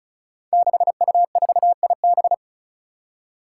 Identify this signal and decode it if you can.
Morse code, heard as a keyed tone, about 36 wpm, 700 Hz. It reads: 6U4IB
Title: Five Challenging Letter-Number Combinations 35wpm